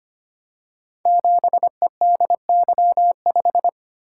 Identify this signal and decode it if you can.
Morse code, heard as a keyed tone, about 25 wpm, 700 Hz. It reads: 7EDY5
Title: Code Groups - 5 Letter:Numbers - 25wpm